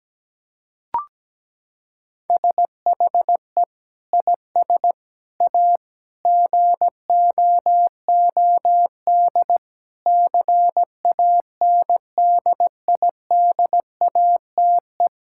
Morse code, heard as a keyed tone, about 17 wpm, 700 Hz. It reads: SHE IS A GOOD CANDIDATE
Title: Sentences from Top 1600 Words 17wpm